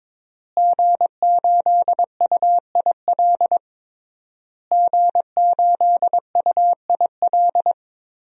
Morse code, heard as a keyed tone, about 21 wpm, 700 Hz. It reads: G8UIL G8UIL
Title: DXCC-100 Call Signs - 5 Characters - Repeated 3x - 22wpm